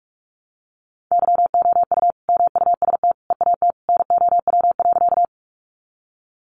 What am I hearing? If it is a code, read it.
Morse code, heard as a keyed tone, about 44 wpm, 700 Hz. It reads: YOU MUST EAT NOW.
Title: Sentences from Top 300 Words 45wpm